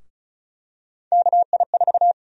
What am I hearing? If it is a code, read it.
Morse code, heard as a keyed tone, about 36 wpm, 700 Hz. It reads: KI4
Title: USA Call Sign Prefixes - Repeated 2x - 35wpm